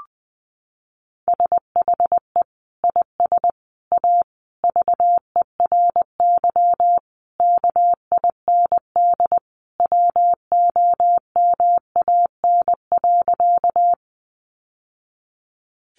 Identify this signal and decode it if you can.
Morse code, heard as a keyed tone, about 20 wpm, 700 Hz. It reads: SHE IS A VERY KIND WOMAN.
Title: Sentences from Top 1100 Words 20wpm